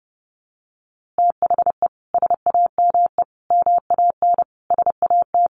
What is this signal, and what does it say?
Morse code, heard as a keyed tone, about 30 wpm, 700 Hz. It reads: THE SAME MAN SAT
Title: A Sound Of Thunder 30wpm